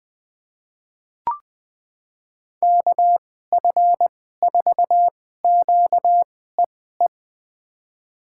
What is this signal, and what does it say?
Morse code, heard as a keyed tone, about 20 wpm, spaced as for 14 wpm, 700 Hz. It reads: KF4QEE
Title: USA Call Signs - 6 Characters - 2x Character Spacing - 20wpm